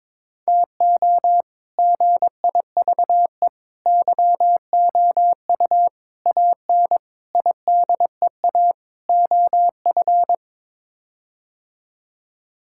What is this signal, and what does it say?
Morse code, heard as a keyed tone, about 22 wpm, 700 Hz. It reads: TO GIVE YOU AN IDEA OF
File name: practice / 22wpm / Sets of 6 Words - 22wpm